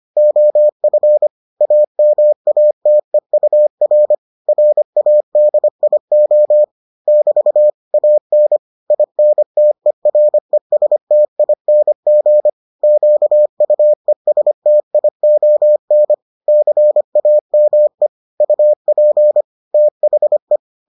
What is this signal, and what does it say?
Morse code, heard as a keyed tone, about 25 wpm, 600 Hz. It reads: OF AMATEUR RADIO <BT> AN INTERESTING QUESTION CAME UP THE